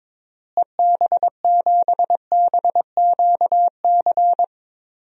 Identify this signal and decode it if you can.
Morse code, heard as a keyed tone, about 22 wpm, 700 Hz. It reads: EB7BQC
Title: DXCC-100 Call Signs - 6 Characters - Repeated 2x - 22wpm